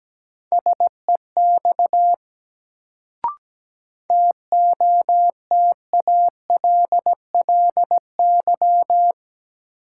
Morse code, heard as a keyed tone, about 17 wpm, 700 Hz. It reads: SEX TOTALLY
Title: Sentences from Top 1600 Words - New Word Drill - 17wpm